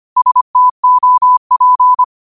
Morse code, heard as a keyed tone, about 25 wpm, 1000 Hz. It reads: STOP
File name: qotdfiles / 25wpm